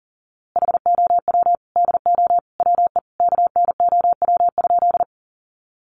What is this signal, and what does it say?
Morse code, heard as a keyed tone, about 40 wpm, 700 Hz. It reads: HOW DO WE KNOW?
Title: A Sound Of Thunder 40wpm